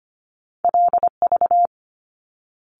Morse code, heard as a keyed tone, about 25 wpm, 700 Hz. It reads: LV
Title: Top Two Letter Combinations in Words 25wpm